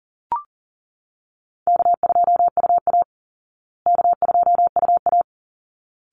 Morse code, heard as a keyed tone, about 40 wpm, 700 Hz. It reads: K2UA K2UA
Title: Easy Call Signs Repeated 2x - 40wpm